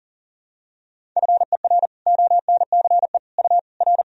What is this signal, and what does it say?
Morse code, heard as a keyed tone, about 40 wpm, 700 Hz. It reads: FER ONCE U R
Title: Sentences from Top 300 Words and Abbreviations 40wpm